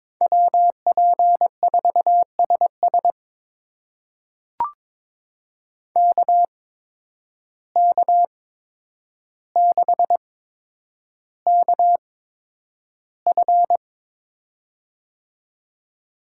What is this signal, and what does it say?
Morse code, heard as a keyed tone, about 23 wpm, spaced as for 10 wpm, 700 Hz. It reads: WP4SS KK6KF
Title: USA Call Signs - 5 Characters - 8x Character Spacing - 22wpm